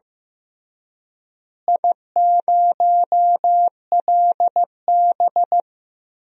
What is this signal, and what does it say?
Morse code, heard as a keyed tone, about 16 wpm, 700 Hz. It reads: I0LB
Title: DXCC-100 Call Signs - 4 Characters - Repeated 3x - 15wpm